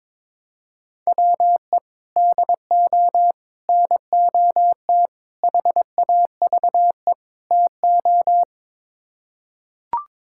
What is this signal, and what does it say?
Morse code, heard as a keyed tone, about 22 wpm, 700 Hz. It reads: WE DO NOT HAVE TO
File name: practice / 22wpm / Build A Phrase - 5-Word Phrases - 22wpm